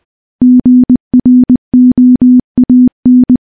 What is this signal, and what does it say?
Morse code, heard as a keyed tone, about 20 wpm, 250 Hz. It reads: GROAN
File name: morse / morse-training-20WPM